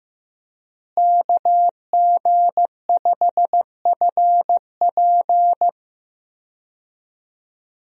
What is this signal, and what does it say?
Morse code, heard as a keyed tone, about 15 wpm, 700 Hz. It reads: KG5FP
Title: USA Call Signs - 5 Characters - Repeated 3x - 15wpm